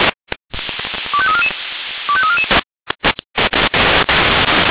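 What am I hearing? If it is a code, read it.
Five-tone selective call (ZVEI), heard as quick runs of five tones, tones of about 70 ms. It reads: 25344 25344
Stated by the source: ZVEI five-tone call received on SDR.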